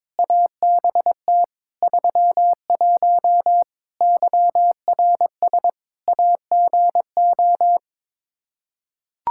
Morse code, heard as a keyed tone, about 22 wpm, 700 Hz. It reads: ABT 31 YRS AGO
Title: K1USN SST QSO Practice 22wpm